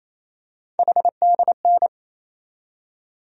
Morse code, heard as a keyed tone, about 28 wpm, 700 Hz. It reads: HDN
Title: USA Call Sign Postfixes - 3 Letters 28wpm